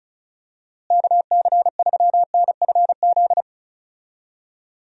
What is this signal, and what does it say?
Morse code, heard as a keyed tone, about 36 wpm, 700 Hz. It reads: KC3NFZ
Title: USA Call Signs - 6 Characters - Repeated 3x - 35wpm